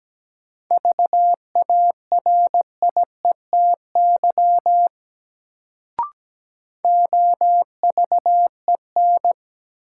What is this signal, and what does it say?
Morse code, heard as a keyed tone, about 17 wpm, 700 Hz. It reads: VARIETY OVEN
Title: Sentences from Top 1000 Words - Review Part 4 - New Word Drill - 17wpm